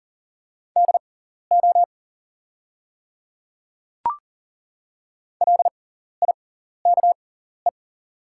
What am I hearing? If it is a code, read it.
Morse code, heard as a keyed tone, about 40 wpm, spaced as for 14 wpm, 700 Hz. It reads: DO LIKE
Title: 6x Character Spacing - Top 100 Words - 40wpm